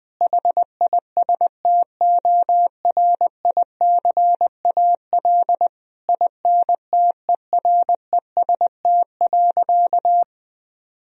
Morse code, heard as a keyed tone, about 21 wpm, 700 Hz. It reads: HISTORICAL INTEREST.
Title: Sentences from Top 1000 Words - Review Part 4 - 20wpm